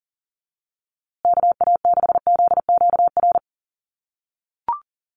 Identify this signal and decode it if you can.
Morse code, heard as a keyed tone, about 40 wpm, 700 Hz. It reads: KA6ZQR
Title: USA Call Signs Repeated 3x 40wpm